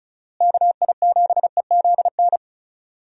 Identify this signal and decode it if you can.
Morse code, heard as a keyed tone, about 34 wpm, 700 Hz. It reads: KI7EZN
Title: USA Call Signs - 6 Characters - Repeated 2x - 35wpm